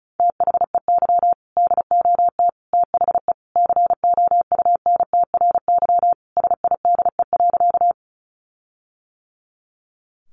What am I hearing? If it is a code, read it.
Morse code, heard as a keyed tone, about 34 wpm, 700 Hz. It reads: THEY DOT THE COUNTRYSIDE.
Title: Sentences from Top 1500 Words 35wpm